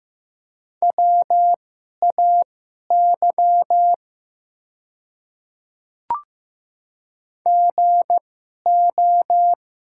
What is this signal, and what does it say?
Morse code, heard as a keyed tone, about 15 wpm, spaced as for 11 wpm, 690 Hz. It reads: WAY GO
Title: 2x Character Spacing - Top 100 Words - 15wpm